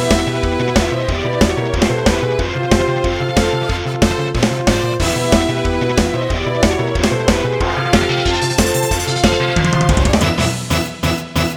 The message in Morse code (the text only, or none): none